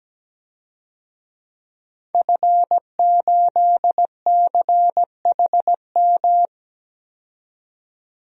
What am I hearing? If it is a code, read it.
Morse code, heard as a keyed tone, about 17 wpm, 700 Hz. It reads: F8CHM
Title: DXCC-100 Call Signs - 5 Characters - 17wpm